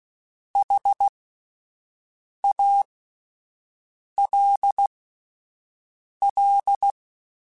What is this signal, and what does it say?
Morse code, heard as a keyed tone, about 16 wpm, spaced as for 5 wpm, 800 Hz. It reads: HALL
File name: Alphabetical / Alphabetical mp3s / Hs @5